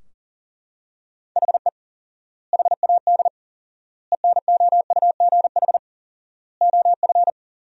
Morse code, heard as a keyed tone, about 40 wpm, 700 Hz. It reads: HE HAD ENOUGH OF
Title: 4x Word-Spacing - Sentences from Top 400 Words - 40wpm